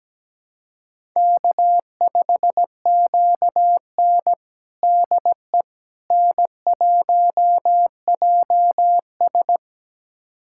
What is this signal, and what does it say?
Morse code, heard as a keyed tone, about 17 wpm, 700 Hz. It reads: K5QN DE N1JS
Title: USA Callsign DE Callsign - 4 Characters - 17wpm